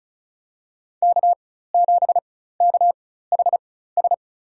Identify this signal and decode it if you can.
Morse code, heard as a keyed tone, about 34 wpm, spaced as for 16 wpm, 700 Hz. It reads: K7KHS
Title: USA Call Signs - 5 Characters - 4x Character Spacing - 35wpm